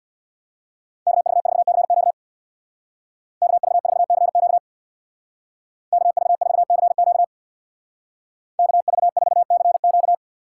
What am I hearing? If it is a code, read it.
Morse code, heard as a keyed tone, about 62 wpm, 700 Hz. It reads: KV4CQ KV4CQ KV4CQ KV4CQ
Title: Speed Racing - USA Call Signs 50wpm